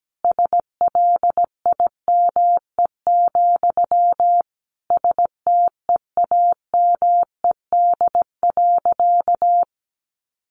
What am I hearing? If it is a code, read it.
Morse code, heard as a keyed tone, about 17 wpm, 700 Hz. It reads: SLIME, STEAMED.
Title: A Sound Of Thunder 17wpm